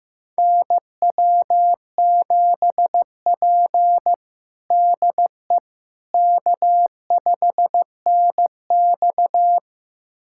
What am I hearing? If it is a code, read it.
Morse code, heard as a keyed tone, about 15 wpm, 700 Hz. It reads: NW7P DE K5NX
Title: USA Callsign DE Callsign - 4 Characters - 15wpm